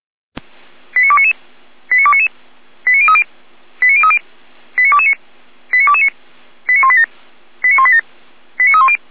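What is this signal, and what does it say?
Five-tone selective call (ZVEI), heard as quick runs of five tones, tones of about 70 ms. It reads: 89299 89299 89030 89030 89229 89229 89187 89187 89310
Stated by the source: ZVEI five-tone selective call received by SDR.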